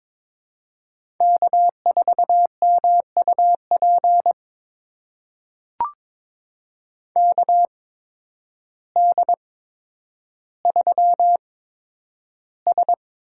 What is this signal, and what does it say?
Morse code, heard as a keyed tone, about 22 wpm, spaced as for 10 wpm, 700 Hz. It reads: K4MUP KD3S
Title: USA Call Signs - 5 Characters - 8x Character Spacing - 22wpm